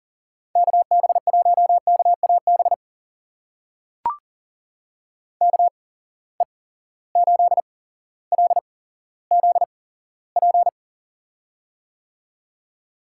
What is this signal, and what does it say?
Morse code, heard as a keyed tone, about 40 wpm, spaced as for 17 wpm, 700 Hz. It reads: KB1KAB KE8LZP
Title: USA Call Signs - 6 Characters - 8x Character Spacing - 40wpm